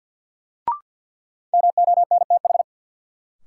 Morse code, heard as a keyed tone, about 46 wpm, 700 Hz. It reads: MONTH
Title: Sets of 5 Words Encore - New Word Drill - 50wpm